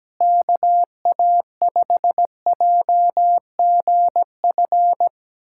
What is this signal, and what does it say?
Morse code, heard as a keyed tone, about 17 wpm, 700 Hz. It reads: KA5JGF
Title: USA Callsign DE Callsign - 6 Characters - 17wpm